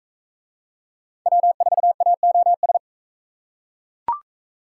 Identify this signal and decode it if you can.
Morse code, heard as a keyed tone, about 44 wpm, 700 Hz. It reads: W4AOS